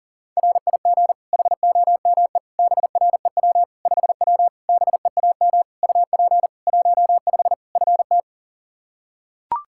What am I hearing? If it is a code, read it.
Morse code, heard as a keyed tone, about 40 wpm, 700 Hz. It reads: RIG HOME BREW 5W BEAM UP 15 FT
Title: K1USN SST QSO Practice 40wpm